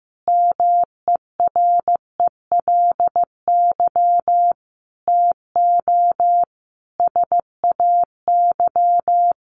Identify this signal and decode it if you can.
Morse code, heard as a keyed tone, about 15 wpm, 700 Hz. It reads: MERELY TO SAY
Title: Sentences from Top 1400 Words 15wpm